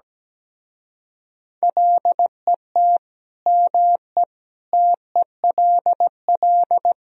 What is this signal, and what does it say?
Morse code, heard as a keyed tone, about 17 wpm, 700 Hz. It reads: LET ME TELL
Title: Speed-Racing - Sets of 5 Words 17wpm